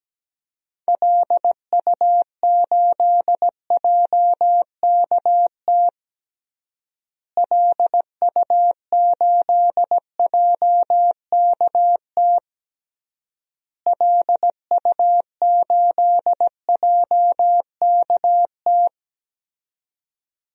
Morse code, heard as a keyed tone, about 17 wpm, 700 Hz. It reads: LU8JKT LU8JKT LU8JKT
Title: DXCC-100 Call Signs - 6 Characters - Repeated 3x - 17wpm